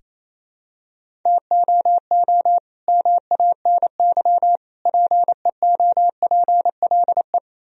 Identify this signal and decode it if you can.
Morse code, heard as a keyed tone, about 28 wpm, 700 Hz. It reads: TOO MANY PEOPLE
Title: Sentences from Top 1000 Words - Review Part 1 - 28wpm